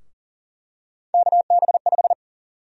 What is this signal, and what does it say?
Morse code, heard as a keyed tone, about 40 wpm, 700 Hz. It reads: KB5
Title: USA Call Sign Prefixes - Repeated 2x - 40wpm